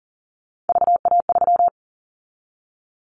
Morse code, heard as a keyed tone, about 38 wpm, 700 Hz. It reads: VA3
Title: Callsign Prefixes for Top 25 DXCC - 40wpm